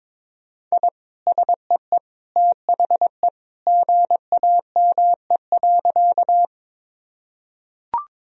Morse code, heard as a keyed tone, about 22 wpm, 700 Hz. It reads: I SEE THE GAME.
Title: Sentences from Top 1000 Words - Review Part 3 - 22wpm